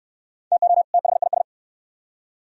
Morse code, heard as a keyed tone, about 62 wpm, 700 Hz. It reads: TO THIS